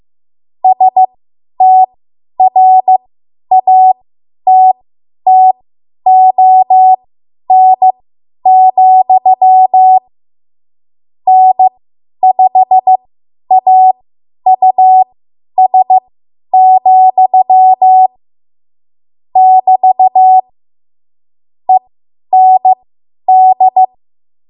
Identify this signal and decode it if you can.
Morse code, heard as a keyed tone, about 15 wpm, spaced as for 10 wpm, 750 Hz. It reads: STRATTON, N5AUS, <BT> END
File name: Morse / 10WPM